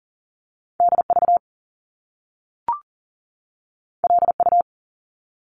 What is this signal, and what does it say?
Morse code, heard as a keyed tone, about 40 wpm, 700 Hz. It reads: DV LU
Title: Top Two Letter Combinations in Words 40wpm